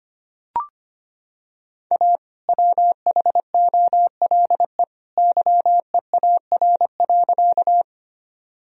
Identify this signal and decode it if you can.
Morse code, heard as a keyed tone, about 25 wpm, 700 Hz. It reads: A WHOLE YEAR.
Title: Sentences from Top 1000 Words - Review Part 1 - 25wpm